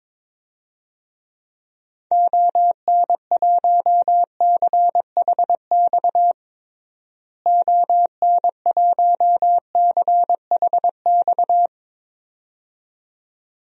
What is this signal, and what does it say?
Morse code, heard as a keyed tone, about 22 wpm, 700 Hz. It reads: ON1CHX ON1CHX
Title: DXCC-100 Call Signs - 6 Characters - Repeated 2x - 22wpm